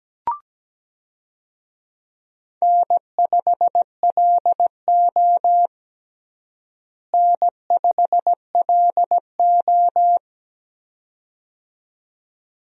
Morse code, heard as a keyed tone, about 17 wpm, 700 Hz. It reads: N5LO N5LO
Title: USA Call Signs - 4 Characters - Repeated 2x - 17wpm